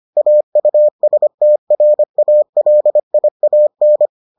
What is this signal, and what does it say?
Morse code, heard as a keyed tone, about 25 wpm, 600 Hz. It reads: AUSTRALIAN